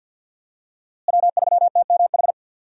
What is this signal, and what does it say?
Morse code, heard as a keyed tone, about 47 wpm, 700 Hz. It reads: W3TMH
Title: USA Call Signs - 5 Characters - Repeated 2x - 50wpm